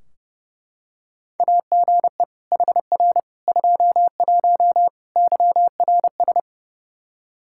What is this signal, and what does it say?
Morse code, heard as a keyed tone, about 28 wpm, 700 Hz. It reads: AGE HR 21 YRS
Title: Practice QSO - Exchange 3 - More Info - 30wpm